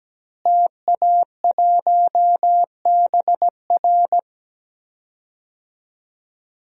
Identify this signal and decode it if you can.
Morse code, heard as a keyed tone, about 17 wpm, 700 Hz. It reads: TA1BR
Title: DXCC-100 Call Signs - 5 Characters - Repeated 2x - 17wpm